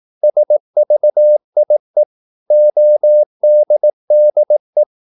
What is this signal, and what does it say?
Morse code, heard as a keyed tone, about 18 wpm, 600 Hz. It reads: SVIE ODDE
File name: Korte ord DK 18wpm